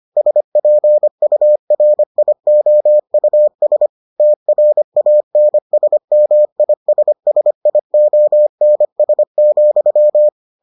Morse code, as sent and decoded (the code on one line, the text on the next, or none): ... .--. ..- .-. .. --- ..- ... / - .-. .- -. ... -- .. ... ... .. --- -. ... --..--
SPURIOUS TRANSMISSIONS,